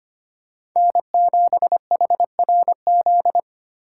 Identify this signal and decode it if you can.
Morse code, heard as a keyed tone, about 25 wpm, 700 Hz. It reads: N7HRZ
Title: USA Call Signs - 4x Character Spacing - 25wpm